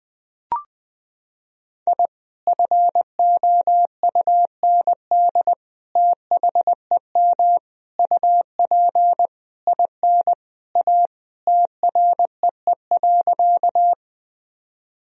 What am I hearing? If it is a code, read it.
Morse code, heard as a keyed tone, about 20 wpm, 700 Hz. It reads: I FOUND THEM UP IN A TREE.
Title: Sentences from Top 1000 Words - Review Part 1 - 20wpm